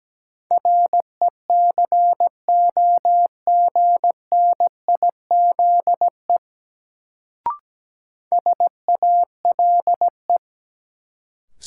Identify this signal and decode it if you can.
Morse code, heard as a keyed tone, about 17 wpm, 700 Hz. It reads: RECOGNIZE SALE
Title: Sentences from Top 1400 Words - New Word Drill - 17wpm